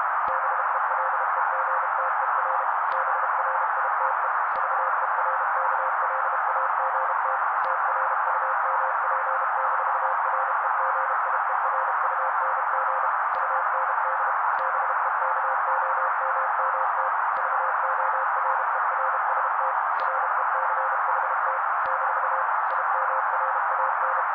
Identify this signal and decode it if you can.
Morse code, heard as a keyed tone, about 32 wpm, 530 Hz. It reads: BELEGTER BRITE VERKLAGT TRUMP-REGIERUNG UND BEKOMMT VORERST RECHT - IMRAN
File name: daily / 31wpm noise